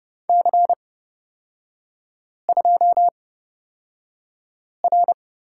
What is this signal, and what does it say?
Morse code, heard as a keyed tone, about 30 wpm, spaced as for 5 wpm, 700 Hz. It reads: C2R
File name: Lesson 36e - BT - Three Characters 30wpm